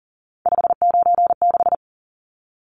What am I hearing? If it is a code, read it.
Morse code, heard as a keyed tone, about 40 wpm, 700 Hz. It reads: 596
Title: Numbers - 3-Digit 40wpm